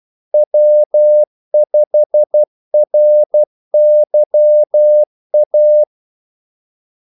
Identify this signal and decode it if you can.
Morse code, heard as a keyed tone, about 12 wpm, 600 Hz. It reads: W5RYA
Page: 2 ekstra ordmellemrum mellem kaldesignalerne